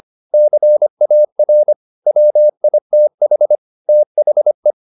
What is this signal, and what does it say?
Morse code, heard as a keyed tone, about 25 wpm, 600 Hz. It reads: CAR WITH THE